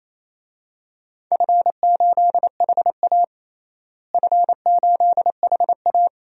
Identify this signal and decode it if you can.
Morse code, heard as a keyed tone, about 28 wpm, 700 Hz. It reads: F8HA F8HA
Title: DXCC-100 Call Signs - 4 Characters - Repeated 2x - 28wpm